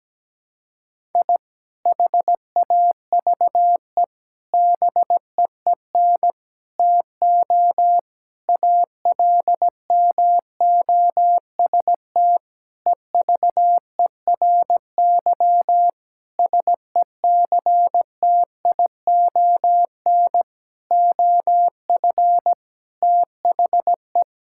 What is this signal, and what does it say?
Morse code, heard as a keyed tone, about 17 wpm, 700 Hz. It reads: I HAVE BEEN TO ALMOST EVERY SECTION OF THE
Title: Sentences from Top 1000 Words - Review Part 5 - 17wpm